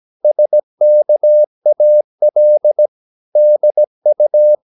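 Morse code, heard as a keyed tone, about 17 wpm, 600 Hz. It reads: SKAL DU
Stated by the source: Slow (QRS) Morse practice at 17 wpm, from CW med Gnister.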